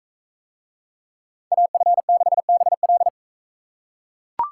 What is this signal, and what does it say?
Morse code, heard as a keyed tone, about 41 wpm, 700 Hz. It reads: AF6BL